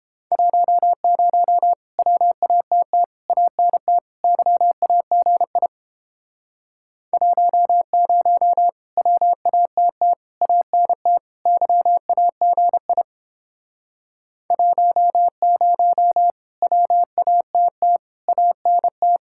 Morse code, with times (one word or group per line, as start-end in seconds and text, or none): 0.3-1.7 10
2.0-3.0 WATT
3.3-4.0 ANT
4.2-5.7 YAGI
7.1-8.7 10
9.0-10.1 WATT
10.4-11.2 ANT
11.5-13.0 YAGI
14.5-16.3 10
16.6-18.0 WATT
18.3-19.2 ANT